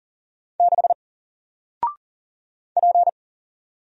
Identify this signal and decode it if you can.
Morse code, heard as a keyed tone, about 40 wpm, 700 Hz. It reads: 6 P
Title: Single Character (Letter:Number:Punctuation:Prosign) Alt 40wpm